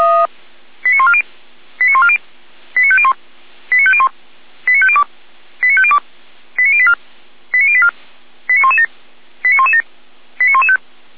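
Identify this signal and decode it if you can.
Five-tone selective call (ZVEI), heard as quick runs of five tones, tones of about 70 ms. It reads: 89140 89140 89501 89501 89502 89502 89084 89084 89117 89117 89115
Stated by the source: ZVEI five-tone call 89120 received on an SDR, from the Feuerwehr Orlamünde audio server.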